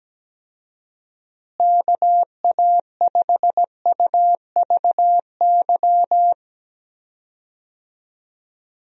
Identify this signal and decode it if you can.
Morse code, heard as a keyed tone, about 17 wpm, 700 Hz. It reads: KA5UVY
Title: USA Call Signs - 6 Characters - Repeated 2x - 17wpm